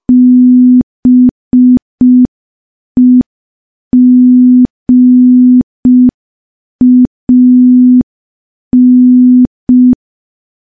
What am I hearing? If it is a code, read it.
Morse code, heard as a keyed tone, about 5 wpm, 250 Hz. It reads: BEGAN
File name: morse / morse-training-05WPM